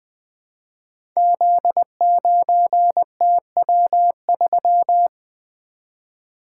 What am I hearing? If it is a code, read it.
Morse code, heard as a keyed tone, about 20 wpm, 700 Hz. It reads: Z9TW3
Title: Code Groups - 5 Letter:Numbers - 20wpm